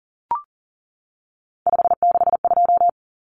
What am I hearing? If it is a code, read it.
Morse code, heard as a keyed tone, about 40 wpm, 700 Hz. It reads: 562
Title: Common Cut Numbers - 3-Digit 40wpm